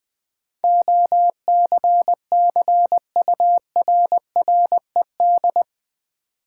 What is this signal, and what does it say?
Morse code, heard as a keyed tone, about 20 wpm, 700 Hz. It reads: OCCURRED
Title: Sets of 5 Words Encore - New Word Drill - 20wpm